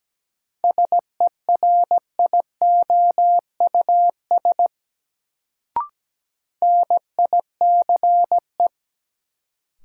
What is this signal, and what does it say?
Morse code, heard as a keyed tone, about 17 wpm, 700 Hz. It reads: SERIOUS NICE
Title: Sentences from Top 1100 Words - New Word Drill - 17wpm